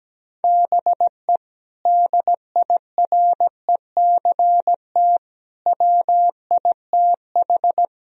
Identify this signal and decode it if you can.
Morse code, heard as a keyed tone, about 17 wpm, 700 Hz. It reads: BE DIRECT WITH
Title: Sentences from Top 1000 Words - Review Part 2 - 17wpm